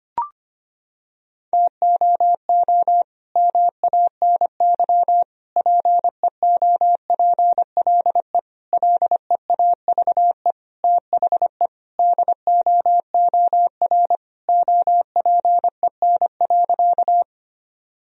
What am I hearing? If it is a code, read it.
Morse code, heard as a keyed tone, about 25 wpm, 700 Hz. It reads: TOO MANY PEOPLE LEAVE THE DOOR OPEN.
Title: Sentences from Top 1000 Words - Review Part 1 - 25wpm